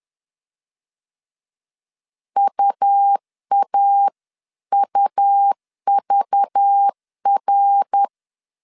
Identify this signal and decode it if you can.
Morse code, heard as a keyed tone, about 10 wpm, 800 Hz. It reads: UAUVR
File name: CW / DET / 1 9 1500C S A 10wpm 4511